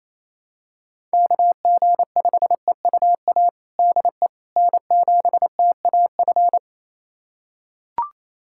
Morse code, heard as a keyed tone, about 27 wpm, 700 Hz. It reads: KG5EUA DE N7TAF
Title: USA Callsign DE Callsign 28wpm